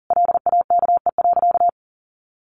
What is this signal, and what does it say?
Morse code, heard as a keyed tone, about 40 wpm, 700 Hz. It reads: LAKE.
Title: Sentences from Top 1400 Words 40wpm